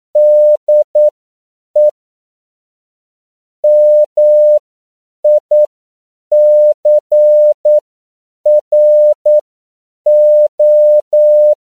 Morse code, heard as a keyed tone, about 9 wpm, spaced as for 7 wpm, 600 Hz. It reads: DE MICRO
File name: CW / 10PPM